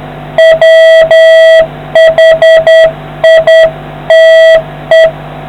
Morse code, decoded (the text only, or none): WHITE